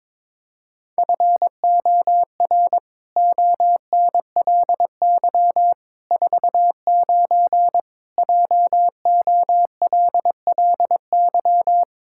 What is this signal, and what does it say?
Morse code, heard as a keyed tone, about 22 wpm, 700 Hz. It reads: FOR ONLY 49 JOLLY
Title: Pangram Challenge v2 - 22wpm